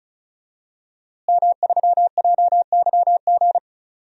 Morse code, heard as a keyed tone, about 36 wpm, 700 Hz. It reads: M3JYG
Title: DXCC-100 Call Signs - 5 Characters - Repeated 3x - 35wpm